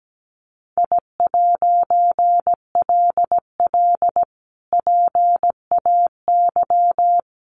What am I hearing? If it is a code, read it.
Morse code, heard as a keyed tone, about 17 wpm, 700 Hz. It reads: I'LL PAY
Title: A Sound Of Thunder 17wpm